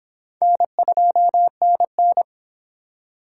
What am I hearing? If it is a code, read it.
Morse code, heard as a keyed tone, about 24 wpm, 700 Hz. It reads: N2NN